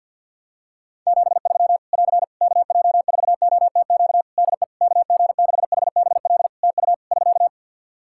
Morse code, heard as a keyed tone, about 51 wpm, 700 Hz. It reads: 73 <AR> KJ4OTQ DE KG6HBL TU <SK>
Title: Practice QSO - Exchange 4 - Ending - 50wpm